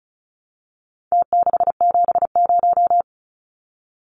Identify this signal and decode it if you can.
Morse code, heard as a keyed tone, about 34 wpm, 700 Hz. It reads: T670
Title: Numbers - Contest Style - 4-Digit 35wpm